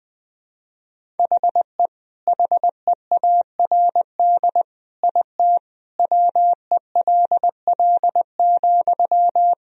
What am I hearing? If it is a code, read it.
Morse code, heard as a keyed tone, about 21 wpm, 700 Hz. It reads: HE HEARD IT WELL,
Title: Sentences from Top 1000 Words - Review Part 3 - 20wpm